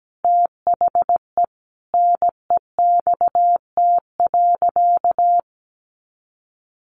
Morse code, heard as a keyed tone, about 17 wpm, 700 Hz. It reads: THE NEXT.
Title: Sentences from Top 1500 Words 17wpm